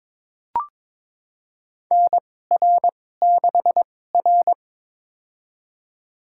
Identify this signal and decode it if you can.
Morse code, heard as a keyed tone, about 22 wpm, spaced as for 16 wpm, 700 Hz. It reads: NR6R
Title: USA Call Signs - 4 Characters - 2x Character Spacing - 22wpm